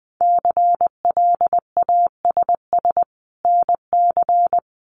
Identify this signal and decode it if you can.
Morse code, heard as a keyed tone, about 20 wpm, 700 Hz. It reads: CLASS NC
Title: Winter Field Day Exchanges 20wpm